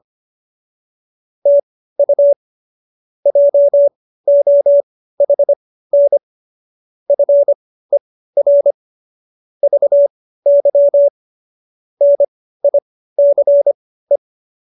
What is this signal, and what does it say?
Morse code, heard as a keyed tone, about 25 wpm, spaced as for 15 wpm, 570 Hz. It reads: TU JOHN FER VY NICE